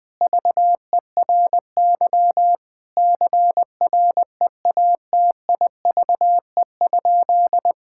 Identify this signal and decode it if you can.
Morse code, heard as a keyed tone, about 20 wpm, 700 Hz. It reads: VERY CREATIVE?
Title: Sentences from Top 1000 Words - Review Part 5 - 20wpm